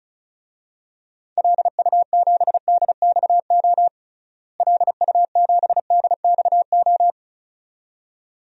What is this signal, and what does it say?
Morse code, heard as a keyed tone, about 36 wpm, 700 Hz. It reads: LU7DXO LU7DXO
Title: DXCC-100 Call Signs - 6 Characters - Repeated 2x - 35wpm